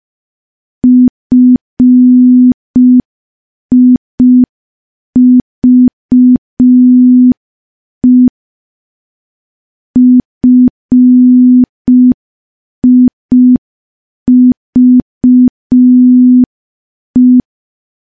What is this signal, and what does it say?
Morse code, heard as a keyed tone, about 5 wpm, 250 Hz. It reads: FIVE FIVE
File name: morse / morse-training-05WPM